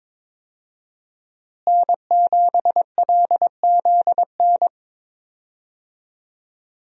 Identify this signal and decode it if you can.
Morse code, heard as a keyed tone, about 22 wpm, 700 Hz. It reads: N7LZN
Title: USA Call Signs - 5 Characters - Repeated 2x - 22wpm